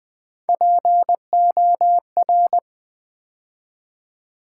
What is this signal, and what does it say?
Morse code, heard as a keyed tone, about 20 wpm, 700 Hz. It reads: POR
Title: Build A Word - 3 Letters At A Time - 9-Letter Words - 20wpm